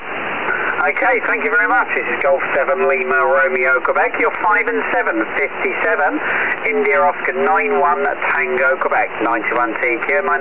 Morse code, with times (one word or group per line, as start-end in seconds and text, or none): none